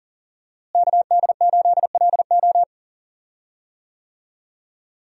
Morse code, heard as a keyed tone, about 40 wpm, 690 Hz. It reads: KD8LO
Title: USA Call Signs - 5 Characters - Repeated 3x - 40wpm